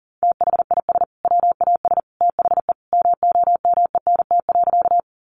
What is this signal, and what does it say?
Morse code, heard as a keyed tone, about 40 wpm, 700 Hz. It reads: THIS WAS THE MOMENT.
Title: Sentences from Top 1000 Words 40wpm